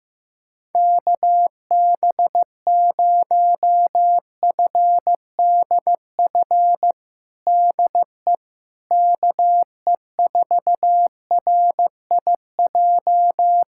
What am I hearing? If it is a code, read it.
Morse code, heard as a keyed tone, about 16 wpm, 700 Hz. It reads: KB0FDF DE KE4RIJ
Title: USA Callsign DE Callsign 15wpm